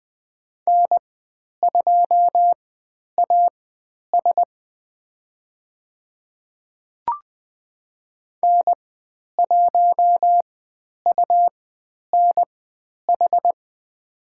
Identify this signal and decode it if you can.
Morse code, heard as a keyed tone, about 20 wpm, spaced as for 10 wpm, 700 Hz. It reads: N2AS N1UNH
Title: 102 Lesson 17g - Letter u - Call Signs 20wpm